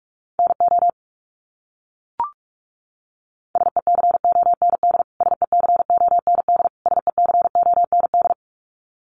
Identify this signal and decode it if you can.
Morse code, heard as a keyed tone, about 44 wpm, 700 Hz. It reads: NO SECOND SECOND SECOND
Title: Top 500 Words Repeated 45wpm